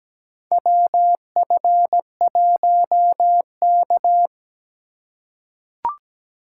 Morse code, heard as a keyed tone, about 17 wpm, 700 Hz. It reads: WF1K
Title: USA Call Signs - 4 Characters - 3x Character Spacing - 17wpm